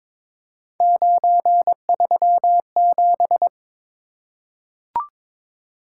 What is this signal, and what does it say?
Morse code, heard as a keyed tone, about 22 wpm, 700 Hz. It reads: 937
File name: practice / 22wpm / Three Letter-Number Combinations - 22wpm